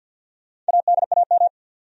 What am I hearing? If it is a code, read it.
Morse code, heard as a keyed tone, about 53 wpm, 700 Hz. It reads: ADAM
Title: USA Common Names - Max 5 Letters - 50wpm